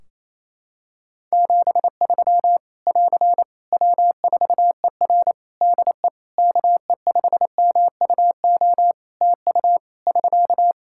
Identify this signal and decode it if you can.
Morse code, heard as a keyed tone, about 28 wpm, 700 Hz. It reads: 73 <AR> W4ER DE KE5MUO TU <SK>
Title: K1USN SST QSO Practice 28wpm